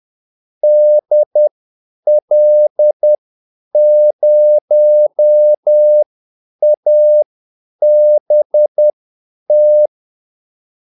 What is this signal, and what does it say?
Morse code, heard as a keyed tone, about 10 wpm, spaced as for 8 wpm, 600 Hz. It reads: DL0ABT
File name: Callsigns-10-8wpm